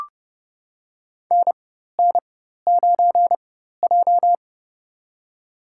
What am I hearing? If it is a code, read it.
Morse code, heard as a keyed tone, about 28 wpm, spaced as for 14 wpm, 700 Hz. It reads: NN9J
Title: USA Call Signs - 4 Characters - 4x Character Spacing - 30wpm